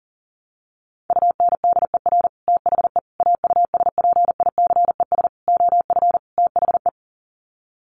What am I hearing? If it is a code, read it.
Morse code, heard as a keyed tone, about 40 wpm, 700 Hz. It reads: UNDER THE AUSPICES OF THE
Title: Sets of 5 Words Encore2 40wpm